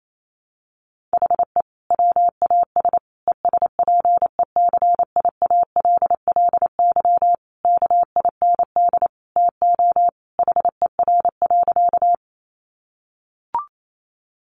Morse code, heard as a keyed tone, about 28 wpm, 700 Hz. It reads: HE WAS ESPECIALLY KIND TO HER.
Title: Sentences from Top 700 Words 28wpm